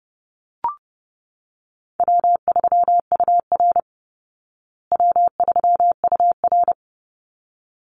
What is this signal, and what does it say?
Morse code, heard as a keyed tone, about 30 wpm, 700 Hz. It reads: W3UR W3UR
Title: Call Signs Repeated 3x - 30wpm